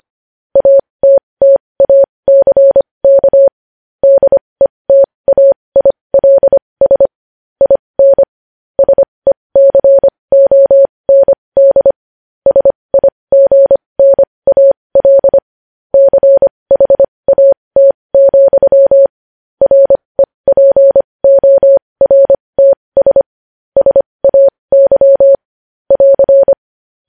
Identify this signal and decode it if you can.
Morse code, heard as a keyed tone, about 25 wpm, spaced as for 20 wpm, 550 Hz. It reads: ATTACK DETAILS IN SECOND SIGNAL CHAT, REPORTS SAY <AR>
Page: News Headlines in Morse Code at 20 WPM